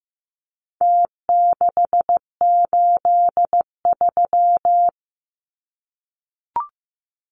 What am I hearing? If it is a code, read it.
Morse code, heard as a keyed tone, about 15 wpm, 700 Hz. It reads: T683
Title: Numbers - Contest Style - 4-Digit 15wpm